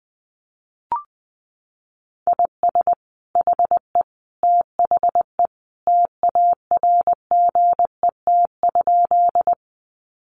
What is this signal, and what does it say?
Morse code, heard as a keyed tone, about 19 wpm, 700 Hz. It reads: IS HE THE TARGET?
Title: Sentences from Top 1200 Words 20wpm